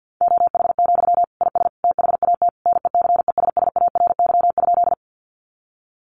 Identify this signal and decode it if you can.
Morse code, heard as a keyed tone, about 50 wpm, 700 Hz. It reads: OH, IS THAT NECESSARY?
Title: Sentences from Top 800 Words 50wpm